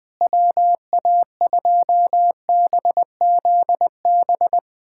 Morse code, heard as a keyed tone, about 21 wpm, 700 Hz. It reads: WA2BZB
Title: USA Callsign DE Callsign - 6 Characters - 20wpm